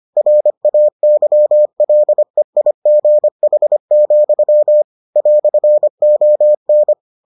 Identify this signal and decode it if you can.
Morse code, heard as a keyed tone, about 25 wpm, 600 Hz. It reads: RAYLEIGH, "ON